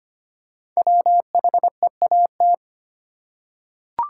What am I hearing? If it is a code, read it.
Morse code, heard as a keyed tone, about 25 wpm, 700 Hz. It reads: WHEAT
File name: 5 Letter Words - First 3 Letters Given - 25wpm